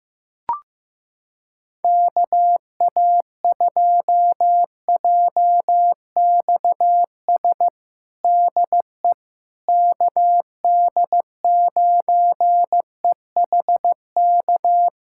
Morse code, heard as a keyed tone, about 16 wpm, 700 Hz. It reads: KA2JXS DE KD9EHK
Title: USA Callsign DE Callsign - 6 Characters - 15wpm